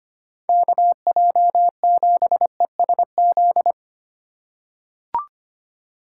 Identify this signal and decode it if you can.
Morse code, heard as a keyed tone, about 25 wpm, 700 Hz. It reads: KJ7ESZ
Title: USA Call Signs - 3x Character Spacing - 25wpm